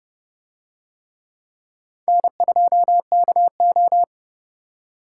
Morse code, heard as a keyed tone, about 29 wpm, 700 Hz. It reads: N2KO